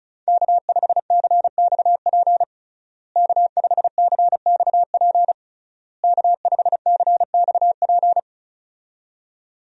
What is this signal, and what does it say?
Morse code, heard as a keyed tone, about 36 wpm, 700 Hz. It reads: K5CXP K5CXP K5CXP
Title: USA Call Signs - 5 Characters - Repeated 3x - 35wpm